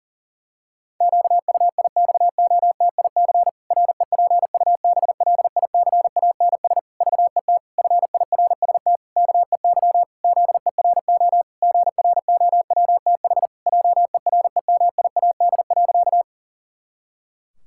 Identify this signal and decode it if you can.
Morse code, heard as a keyed tone, about 37 wpm, 700 Hz. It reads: QUIXOTIC REPUBLICANS VET FIRST KEY ZERO GROWTH JEREMIAD.
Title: Pangram Challenge v2 - 40wpm